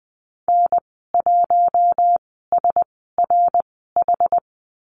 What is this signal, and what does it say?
Morse code, heard as a keyed tone, about 20 wpm, spaced as for 15 wpm, 700 Hz. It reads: N1SRH
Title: 173 Lesson 28g - Number 9 - Call Signs 20wpm